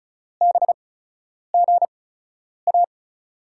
Morse code, heard as a keyed tone, about 36 wpm, spaced as for 10 wpm, 700 Hz. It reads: BGA
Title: USA Call Signs - 8x Character Spacing - 35wpm